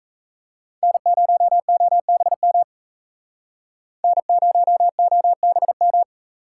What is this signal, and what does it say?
Morse code, heard as a keyed tone, about 35 wpm, 700 Hz. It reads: N0OBM N0OBM